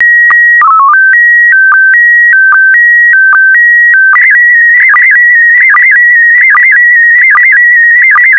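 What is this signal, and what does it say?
SSTV (slow-scan television), PD 160.
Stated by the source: SSTV in PD160 mode.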